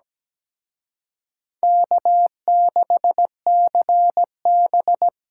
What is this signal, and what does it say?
Morse code, heard as a keyed tone, about 20 wpm, 700 Hz. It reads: K6CB
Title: Speed Racing - USA Call Signs 17wpm